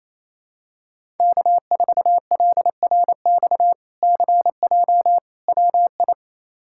Morse code, heard as a keyed tone, about 28 wpm, 700 Hz. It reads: K4LRX CJ WI
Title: K1USN SST QSO Practice - 28wpm